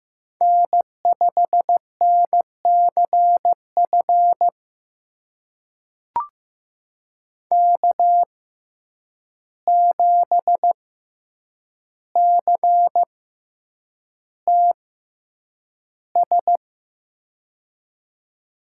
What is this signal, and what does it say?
Morse code, heard as a keyed tone, about 15 wpm, spaced as for 8 wpm, 700 Hz. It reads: N5NCF K7CTS
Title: USA Call Signs - 5 Characters - 6x Character Spacing - 15wpm